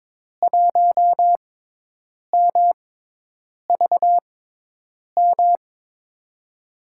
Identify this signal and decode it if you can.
Morse code, heard as a keyed tone, about 22 wpm, spaced as for 8 wpm, 700 Hz. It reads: 1MVM
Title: USA Call Signs - 5 Characters - 6x Character Spacing - 22wpm